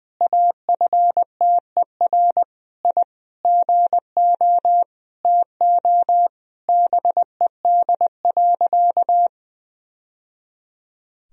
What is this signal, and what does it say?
Morse code, heard as a keyed tone, about 20 wpm, 700 Hz. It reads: AFTER I GO TO BED.
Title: Sentences from Top 1000 Words - Review Part 2 - 20wpm